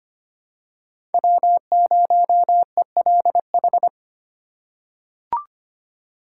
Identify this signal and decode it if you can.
Morse code, heard as a keyed tone, about 25 wpm, 700 Hz. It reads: W0ELH
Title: USA Call Signs - 2x Character Spacing - 25wpm